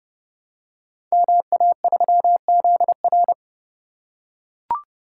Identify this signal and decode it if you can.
Morse code, heard as a keyed tone, about 30 wpm, 700 Hz. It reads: MA3ZR
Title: Code Groups - 5 Letter:Numbers - 30wpm